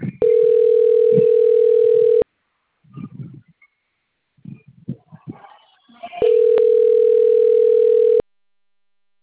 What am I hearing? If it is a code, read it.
Ringback tone (North America), 440 + 480 Hz.